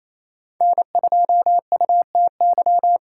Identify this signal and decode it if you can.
Morse code, heard as a keyed tone, about 35 wpm, 700 Hz. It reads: N2UTY
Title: Speed Racing - USA 1x3 Call Signs 28wpm